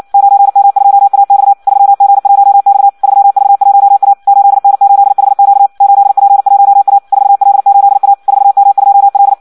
Morse code, heard as a keyed tone, about 56 wpm, 800 Hz. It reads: 9M2AX 3G1X 3V1A 8N8HQ 8P1A 4L8A 4M2L
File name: Audio practica CW / qrq-tx-58 2